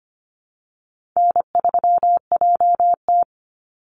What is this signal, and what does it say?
Morse code, heard as a keyed tone, about 25 wpm, 700 Hz. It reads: N3JT
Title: Easy Call Signs Repeated 2x - 25wpm